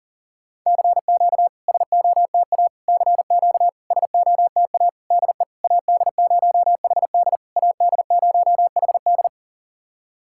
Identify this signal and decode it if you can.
Morse code, heard as a keyed tone, about 40 wpm, 700 Hz. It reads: CQ SOTA CQ SOTA DE AD0HD AD0HD
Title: SOTA QSO Practice 40wpm